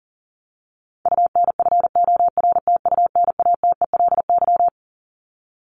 Morse code, heard as a keyed tone, about 37 wpm, 700 Hz. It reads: UNFORTUNATELY
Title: LY Suffix - Top 100 Words with Base Words 40wpm